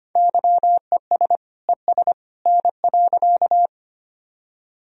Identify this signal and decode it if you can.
Morse code, heard as a keyed tone, about 25 wpm, 700 Hz. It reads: YES ES N.
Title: Sentences from Top 500 Words and Abbreviations 25wpm